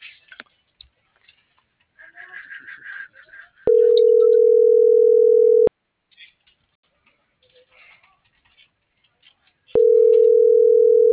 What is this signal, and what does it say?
Ringback tone (North America), 440 + 480 Hz.